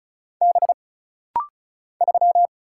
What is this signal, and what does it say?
Morse code, heard as a keyed tone, about 36 wpm, 700 Hz. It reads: B 3
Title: Single Letter-Number - Rapid-Fire 35wpm